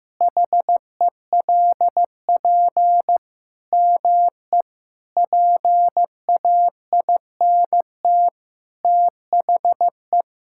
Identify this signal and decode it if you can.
Morse code, heard as a keyed tone, about 16 wpm, 700 Hz. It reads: HELP ME PAINT THE
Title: Sentences from Top 1000 Words - Review Part 6 - 15wpm